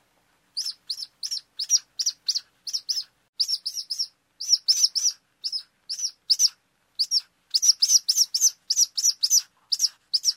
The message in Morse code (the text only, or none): none